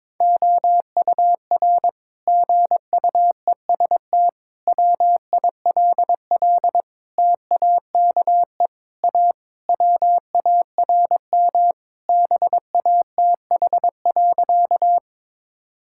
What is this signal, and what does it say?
Morse code, heard as a keyed tone, about 22 wpm, 700 Hz. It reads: OUR GUEST WILL TAKE A WARM BATH.
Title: Sentences from Top 1600 Words 22wpm